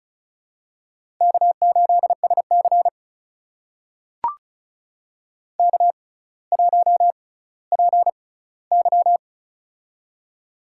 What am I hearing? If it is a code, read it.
Morse code, heard as a keyed tone, about 36 wpm, spaced as for 18 wpm, 700 Hz. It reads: K8SC K1PY
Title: USA Call Signs - 4 Characters - 6x Character Spacing - 35wpm